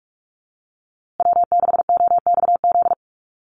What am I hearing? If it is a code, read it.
Morse code, heard as a keyed tone, about 44 wpm, 690 Hz. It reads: W6OXZ
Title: USA Call Signs Repeated 2x 45wpm